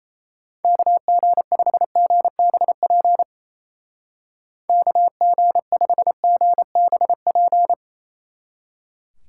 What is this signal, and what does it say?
Morse code, heard as a keyed tone, about 34 wpm, 700 Hz. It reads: KG5GBP KG5GBP
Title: Speed Racing - USA 2x3 Call Signs 28wpm